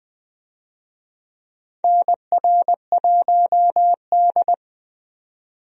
Morse code, heard as a keyed tone, about 20 wpm, 700 Hz. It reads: NR1D
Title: USA Call Signs - 4 Characters - 20wpm